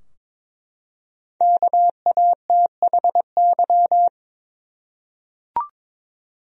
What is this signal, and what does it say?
Morse code, heard as a keyed tone, about 22 wpm, 700 Hz. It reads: KATHY
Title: USA Common Names - 22wpm